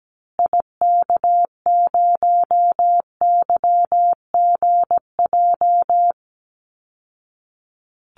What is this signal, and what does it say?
Morse code, heard as a keyed tone, about 17 wpm, 700 Hz. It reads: IK0YGJ
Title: Call Signs Repeated 2x - 17wpm